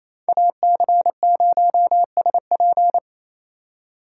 Morse code, heard as a keyed tone, about 28 wpm, 700 Hz. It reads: AC0SP
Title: USA Call Signs - 5 Characters - 2x Character Spacing - 28wpm